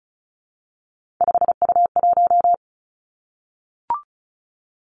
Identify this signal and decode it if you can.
Morse code, heard as a keyed tone, about 36 wpm, 700 Hz. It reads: 5U1
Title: Three Challenging Letter-Number Combinations 35wpm